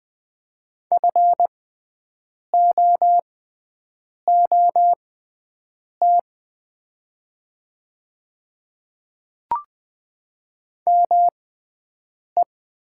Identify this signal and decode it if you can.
Morse code, heard as a keyed tone, about 20 wpm, spaced as for 7 wpm, 700 Hz. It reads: FOOT ME
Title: 6x Character Spacing - Top 500 Words - 20wpm